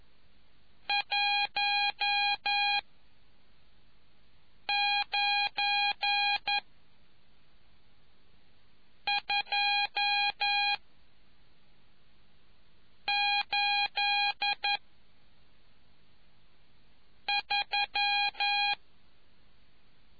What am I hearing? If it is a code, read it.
Morse code, heard as a keyed tone, about 11 wpm, 820 Hz. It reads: 1 9 2 8 3